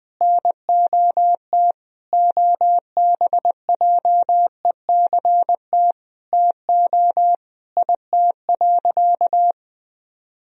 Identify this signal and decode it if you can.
Morse code, heard as a keyed tone, about 20 wpm, 700 Hz. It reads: NOT OBJECT TO IT.
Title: Sentences from Top 1000 Words - Review Part 3 - 20wpm